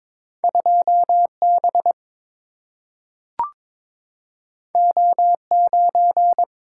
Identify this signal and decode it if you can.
Morse code, heard as a keyed tone, about 22 wpm, 690 Hz. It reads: 2B O9
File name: practice / 22wpm / Two Letter-Number Combinations - 22wpm